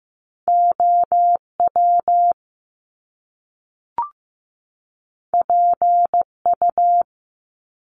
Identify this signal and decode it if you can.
Morse code, heard as a keyed tone, about 15 wpm, 700 Hz. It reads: OW PU
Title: Top Two Letter Combinations in Words 15wpm